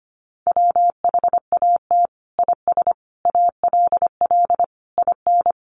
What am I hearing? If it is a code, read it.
Morse code, heard as a keyed tone, about 25 wpm, 700 Hz. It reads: WHAT IS ALL IN
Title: Sentences from Top 1200 Words 25wpm